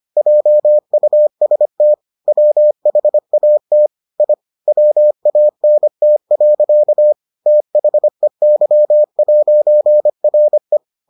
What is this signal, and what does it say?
Morse code, heard as a keyed tone, about 25 wpm, 600 Hz. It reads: JUST WHAT I WANT. THEY'RE